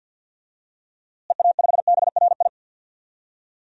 Morse code, heard as a keyed tone, about 46 wpm, 700 Hz. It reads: EA5BRI
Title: DXCC-100 Call Signs - 6 Characters - Repeated 3x - 50wpm